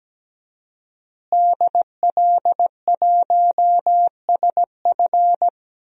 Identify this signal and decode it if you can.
Morse code, heard as a keyed tone, about 17 wpm, 700 Hz. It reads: DL1SF
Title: DXCC-100 Call Signs - 5 Characters - Repeated 2x - 17wpm